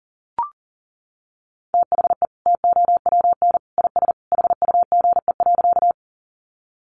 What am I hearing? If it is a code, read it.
Morse code, heard as a keyed tone, about 40 wpm, 700 Hz. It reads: THE TOWN IS HUGE.
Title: Sentences from Top 1100 Words 40wpm